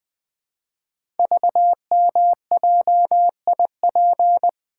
Morse code, heard as a keyed tone, about 20 wpm, 700 Hz. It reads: VMJIP
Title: Code Groups - 5 Letter:Numbers - 20wpm